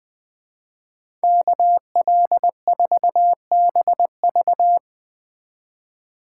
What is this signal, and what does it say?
Morse code, heard as a keyed tone, about 20 wpm, 700 Hz. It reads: KL4BV
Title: Speed Racing - USA 2x2 Call Signs 17wpm